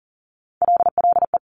Morse code, heard as a keyed tone, about 38 wpm, 700 Hz. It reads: LLE
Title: Top Three Letter Combinations in Words 40wpm